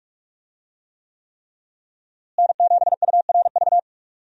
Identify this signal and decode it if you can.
Morse code, heard as a keyed tone, about 42 wpm, 700 Hz. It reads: N7URV
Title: USA Call Signs - 5 Characters - Repeated 3x - 45wpm